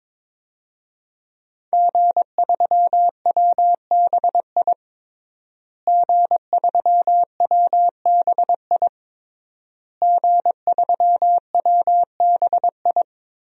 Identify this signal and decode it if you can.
Morse code, heard as a keyed tone, about 22 wpm, 700 Hz. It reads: G3WBI G3WBI G3WBI
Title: DXCC-100 Call Signs - 5 Characters - Repeated 3x - 22wpm